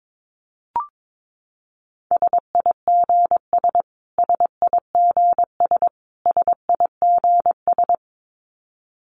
Morse code, heard as a keyed tone, about 22 wpm, 700 Hz. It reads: SIGS SIGS SIGS
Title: Abbreviations Repeated 3x - 22wpm